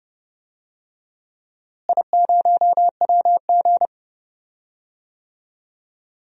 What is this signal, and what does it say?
Morse code, heard as a keyed tone, about 30 wpm, 700 Hz. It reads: I0WG
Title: DXCC-100 Call Signs - 4 Characters - Repeated 2x - 30wpm